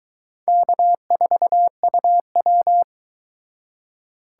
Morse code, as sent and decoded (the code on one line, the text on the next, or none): -.- ....- ..- .--
K4UW